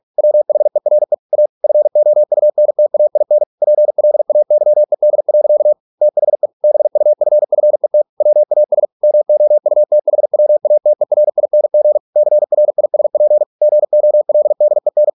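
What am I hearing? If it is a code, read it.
Morse code, heard as a keyed tone, about 46 wpm, 600 Hz. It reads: WHERE A FOUNTAIN PLAYED. THE BUFFET WAS MOUTHWATERING CRISP GOLDEN
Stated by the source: Morse practice at 45 wpm, the novel Never by Ken Follett in English, from CW med Gnister.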